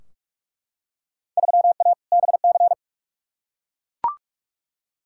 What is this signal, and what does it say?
Morse code, heard as a keyed tone, about 43 wpm, 700 Hz. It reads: 3A BC
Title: Field Day Exchanges - 45wpm